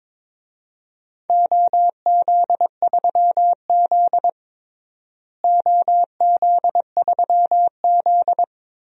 Morse code, heard as a keyed tone, about 22 wpm, 700 Hz. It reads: OZ3Z OZ3Z
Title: DXCC-100 Call Signs - 4 Characters - Repeated 3x - 22wpm